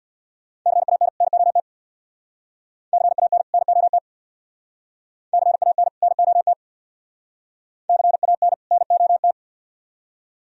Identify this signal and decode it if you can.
Morse code, heard as a keyed tone, about 61 wpm, 700 Hz. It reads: CAN NOT CAN NOT CAN NOT CAN NOT
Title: Speed-Racing - Sets of 2 Words 50wpm